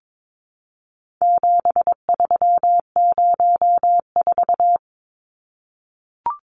Morse code, heard as a keyed tone, about 22 wpm, 700 Hz. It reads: 7304
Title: Numbers - 4-Digit 22wpm